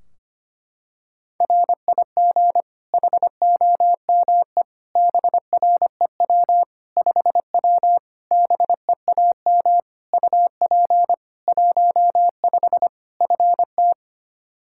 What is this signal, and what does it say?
Morse code, heard as a keyed tone, about 25 wpm, 700 Hz. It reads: RIG HOME BREW 5W BEAM UP 15 FT
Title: K1USN SST QSO Practice 25wpm